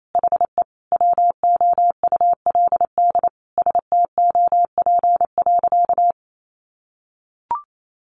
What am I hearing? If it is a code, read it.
Morse code, heard as a keyed tone, about 28 wpm, 700 Hz. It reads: HE WOULD STOP.
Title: Sentences from Top 1300 Words 28wpm